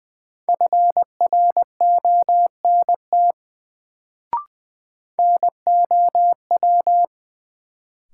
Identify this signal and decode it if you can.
Morse code, heard as a keyed tone, about 20 wpm, 700 Hz. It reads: FRONT NOW
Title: Sets of 2 Words - New Word Drill - 20wpm